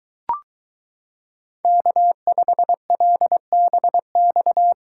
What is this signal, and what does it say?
Morse code, heard as a keyed tone, about 22 wpm, 700 Hz. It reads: K5LBX